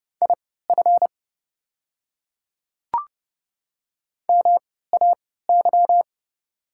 Morse code, heard as a keyed tone, about 29 wpm, spaced as for 17 wpm, 690 Hz. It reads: IF MAY
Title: 3x Character Spacing - Top 300 Words - 30wpm